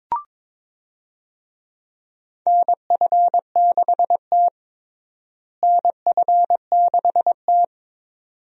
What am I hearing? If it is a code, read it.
Morse code, heard as a keyed tone, about 22 wpm, 700 Hz. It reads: NF6T NF6T
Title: USA Call Signs - 4 Characters - Repeated 2x - 22wpm